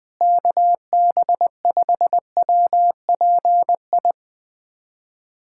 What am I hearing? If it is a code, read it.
Morse code, heard as a keyed tone, about 20 wpm, 700 Hz. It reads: KB5WPI
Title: USA Callsign DE Callsign - 6 Characters - 20wpm